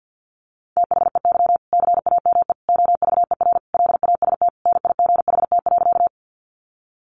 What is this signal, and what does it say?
Morse code, heard as a keyed tone, about 50 wpm, 700 Hz. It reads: THEY CAME OVER LAST NIGHT.
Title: Sentences from Top 300 Words 50wpm